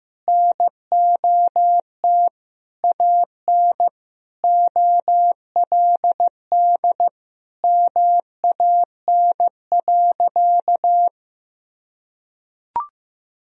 Morse code, heard as a keyed tone, about 15 wpm, 700 Hz. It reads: NOT AN OLD MAN.
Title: Sentences from Top 1000 Words - Review Part 4 - 15wpm